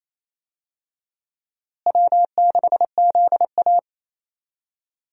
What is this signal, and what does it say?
Morse code, heard as a keyed tone, about 28 wpm, 700 Hz. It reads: W6ZA
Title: USA Call Signs - 4 Characters - Repeated 2x - 28wpm